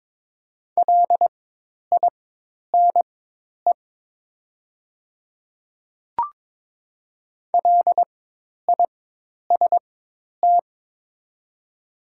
Morse code, heard as a keyed tone, about 22 wpm, spaced as for 10 wpm, 700 Hz. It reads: LINE LIST
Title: 4x Character Spacing - Top 500 Words - 22wpm